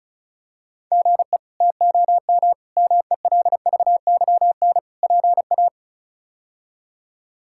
Morse code, heard as a keyed tone, about 34 wpm, 700 Hz. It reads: GE TOM MELVYN PA
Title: K1USN SST QSO Practice - 35wpm